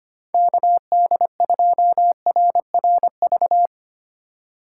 Morse code, heard as a keyed tone, about 25 wpm, 700 Hz. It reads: KD2RRV
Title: USA Call Signs - 6 Characters - 4x Character Spacing - 25wpm